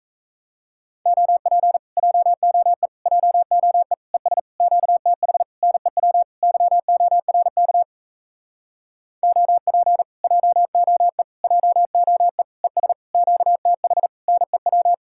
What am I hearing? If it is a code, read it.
Morse code, heard as a keyed tone, about 38 wpm, 700 Hz. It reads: OP JOE JOE ES QTH NEW YORK OP JOE JOE ES QTH NEW